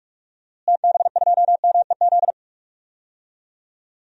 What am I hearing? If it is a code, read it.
Morse code, heard as a keyed tone, about 45 wpm, 700 Hz. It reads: TB2MEZ
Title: DXCC-100 Call Signs - 6 Characters - Repeated 3x - 45wpm